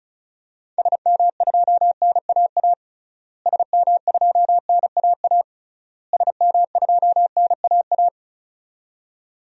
Morse code, heard as a keyed tone, about 34 wpm, 700 Hz. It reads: SM2NAA SM2NAA SM2NAA
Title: DXCC-100 Call Signs - 6 Characters - Repeated 3x - 35wpm